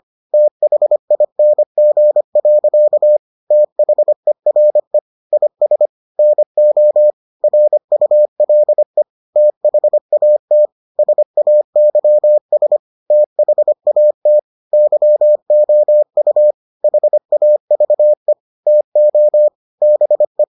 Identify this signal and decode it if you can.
Morse code, heard as a keyed tone, about 25 wpm, 600 Hz. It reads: THING. THERE IS NO RULE THAT SAYS THAT YOU HAVE TO BE